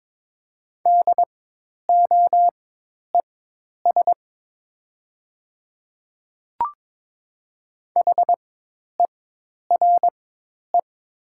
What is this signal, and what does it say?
Morse code, heard as a keyed tone, about 22 wpm, spaced as for 10 wpm, 700 Hz. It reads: DOES HERE
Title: 4x Character Spacing - Top 500 Words - 22wpm